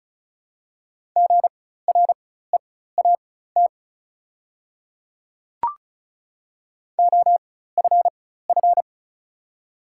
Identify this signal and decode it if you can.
Morse code, heard as a keyed tone, about 36 wpm, spaced as for 16 wpm, 700 Hz. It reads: GREAT OFF
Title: 4x Character Spacing - Top 300 Words - 35wpm